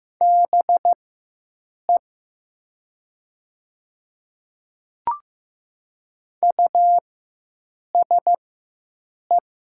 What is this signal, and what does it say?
Morse code, heard as a keyed tone, about 15 wpm, spaced as for 7 wpm, 700 Hz. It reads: BE USE
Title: 4x Character Spacing - Top 300 Words - 15wpm